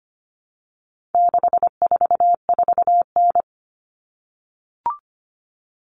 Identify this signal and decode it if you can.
Morse code, heard as a keyed tone, about 25 wpm, 700 Hz. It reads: 644N
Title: All Cut Numbers - 4-Digit 25wpm